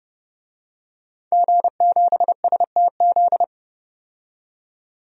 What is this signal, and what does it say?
Morse code, heard as a keyed tone, about 30 wpm, 700 Hz. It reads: G7STZ
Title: DXCC-100 Call Signs - 5 Characters - 30wpm